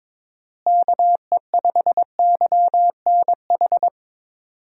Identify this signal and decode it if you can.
Morse code, heard as a keyed tone, about 22 wpm, 700 Hz. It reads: KE5YNH
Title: USA Call Signs - 6 Characters - Repeated 3x - 22wpm